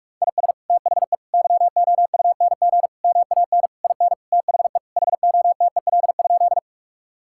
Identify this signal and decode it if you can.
Morse code, heard as a keyed tone, about 41 wpm, 700 Hz. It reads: IS THE YOUNG MAN IN THE HOTEL?
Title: Sentences from Top 1600 Words 45wpm